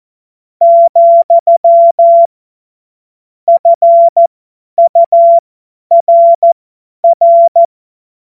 Morse code, heard as a keyed tone, about 14 wpm, spaced as for 10 wpm, 680 Hz. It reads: , FURR
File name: Morsekurs / Witze / Witz 07. September 2021, 50 F70